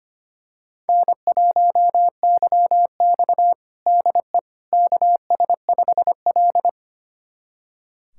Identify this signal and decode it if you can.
Morse code, heard as a keyed tone, about 25 wpm, 700 Hz. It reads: N1YX DE KS5L
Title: USA Callsign DE Callsign - 4 Characters - 25wpm